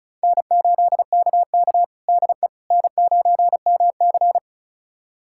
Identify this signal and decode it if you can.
Morse code, heard as a keyed tone, about 36 wpm, 700 Hz. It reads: N8KK DE N9MC
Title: USA Callsign DE Callsign - 4 Characters - 35wpm